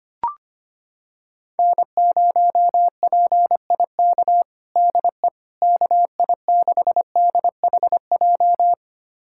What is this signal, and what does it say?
Morse code, heard as a keyed tone, about 25 wpm, 700 Hz. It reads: N0PIK DE KI6DHJ
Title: USA Callsign DE Callsign 25wpm